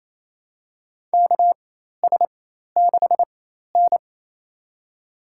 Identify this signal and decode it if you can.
Morse code, heard as a keyed tone, about 28 wpm, spaced as for 13 wpm, 700 Hz. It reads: KS6N
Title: USA Call Signs - 4 Characters - 4x Character Spacing - 28wpm